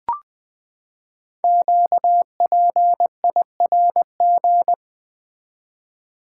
Morse code, heard as a keyed tone, about 20 wpm, 700 Hz. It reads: QPIRG
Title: Code Groups - 5 Letters - 20wpm